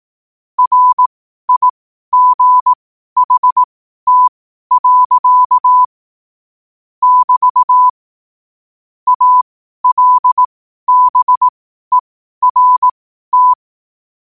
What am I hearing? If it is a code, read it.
Morse code, heard as a keyed tone, about 18 wpm, spaced as for 12 wpm, 1000 Hz. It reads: RIGHT. <BT> ALBERT
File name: qotdfiles / 12wpm